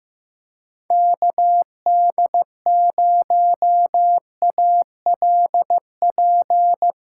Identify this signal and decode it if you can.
Morse code, heard as a keyed tone, about 15 wpm, 700 Hz. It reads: KD0ALP
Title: USA Callsign DE Callsign 15wpm